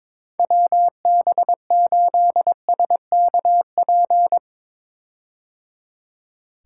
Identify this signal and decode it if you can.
Morse code, heard as a keyed tone, about 22 wpm, 700 Hz. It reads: WB8SKP
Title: K1USN SST QSO Practice - 22wpm